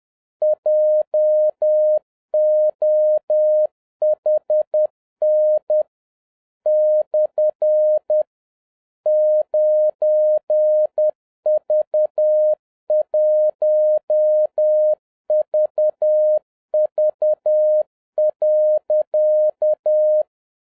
Morse code, heard as a keyed tone, about 10 wpm, 610 Hz. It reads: JOHN / 9V1VV.
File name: 2011-07-mp3 / n21 GREETINGS FROM SINGAPORE 10wpm